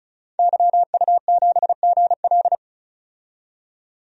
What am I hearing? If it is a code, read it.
Morse code, heard as a keyed tone, about 34 wpm, 700 Hz. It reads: YU7GL
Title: DXCC-100 Call Signs - 5 Characters - 35wpm